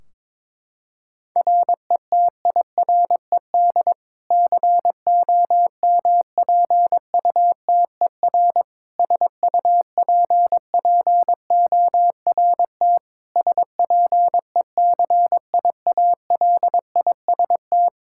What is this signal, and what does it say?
Morse code, heard as a keyed tone, about 22 wpm, 700 Hz. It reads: RETIRED COMPUTER SUPPORT SPECIALIST
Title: Practice QSO - Exchange 3 - More Info - 22wpm